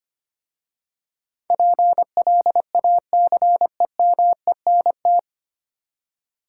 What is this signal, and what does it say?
Morse code, heard as a keyed tone, about 25 wpm, 700 Hz. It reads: PLACEMENT
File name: 9 Letter Words - First 3 Letters Given - 25wpm